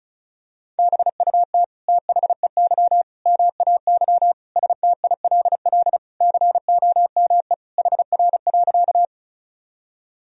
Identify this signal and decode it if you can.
Morse code, heard as a keyed tone, about 36 wpm, 700 Hz. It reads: BUT THEY MAY STILL COME HR.
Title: Sentences from Top 300 Words and Abbreviations 35wpm